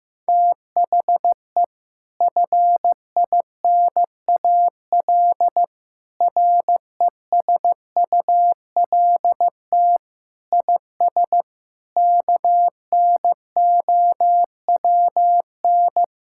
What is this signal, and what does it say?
Morse code, heard as a keyed tone, about 15 wpm, 700 Hz. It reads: THE FINAL RESULT IS KNOWN
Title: Sentences from Top 1000 Words - Review Part 5 - 15wpm